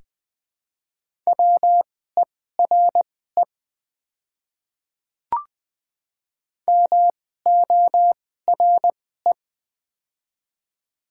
Morse code, heard as a keyed tone, about 20 wpm, spaced as for 14 wpm, 700 Hz. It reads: WERE MORE
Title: 2x Character Spacing - Top 300 Words - 20wpm